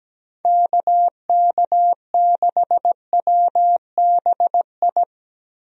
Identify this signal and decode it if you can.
Morse code, heard as a keyed tone, about 17 wpm, 700 Hz. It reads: KK6WBI
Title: USA Call Signs - 6 Characters - Repeated 3x - 17wpm